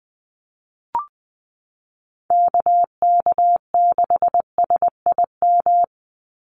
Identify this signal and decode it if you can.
Morse code, heard as a keyed tone, about 20 wpm, 700 Hz. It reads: KK6SIM
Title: USA Call Signs 20wpm